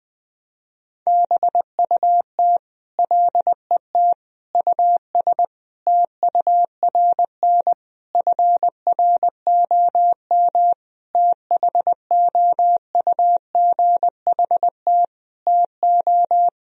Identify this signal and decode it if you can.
Morse code, heard as a keyed tone, about 20 wpm, 700 Hz. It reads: BUT LET US TURN FROM THOUGHT TO
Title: Sentences from Top 1000 Words - Review Part 4 - 20wpm